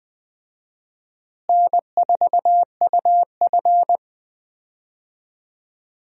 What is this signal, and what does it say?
Morse code, heard as a keyed tone, about 21 wpm, 700 Hz. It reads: N4UF
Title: USA Call Signs - 4 Characters - Repeated 2x - 20wpm